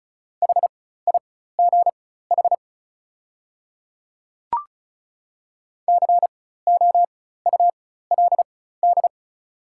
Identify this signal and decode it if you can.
Morse code, heard as a keyed tone, about 34 wpm, spaced as for 16 wpm, 690 Hz. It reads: HIGH COULD
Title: 4x Character Spacing - Top 200 Words - 35wpm